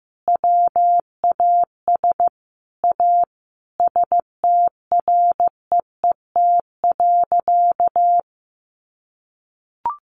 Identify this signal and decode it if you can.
Morse code, heard as a keyed tone, about 15 wpm, 700 Hz. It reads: WAS A STREET.
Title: Sentences from Top 1100 Words 15wpm